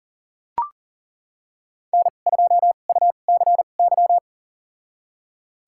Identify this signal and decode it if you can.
Morse code, heard as a keyed tone, about 37 wpm, spaced as for 28 wpm, 700 Hz. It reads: N2UCY
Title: USA Call Signs - 2x Character Spacing - 40wpm